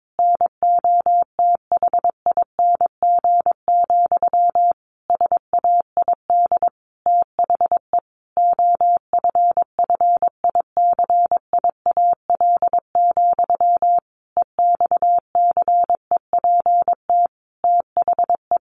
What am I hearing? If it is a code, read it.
Morse code, heard as a keyed tone, about 22 wpm, 700 Hz. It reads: NOTHING, SAID THE OFFICIAL, EXCEPT THE
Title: A Sound Of Thunder 22wpm